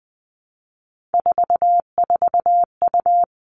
Morse code, heard as a keyed tone, about 20 wpm, 700 Hz. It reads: 44U
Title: All Cut Numbers - 3-Digit 20wpm